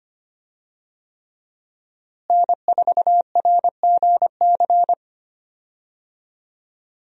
Morse code, heard as a keyed tone, about 25 wpm, 700 Hz. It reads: N4RGC
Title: USA Call Signs - 5 Characters - Repeated 3x - 25wpm